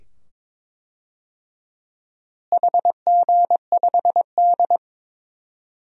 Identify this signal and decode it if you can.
Morse code, heard as a keyed tone, about 22 wpm, 700 Hz. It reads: HG5D
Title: DXCC-100 Call Signs - 4 Characters - Repeated 2x - 22wpm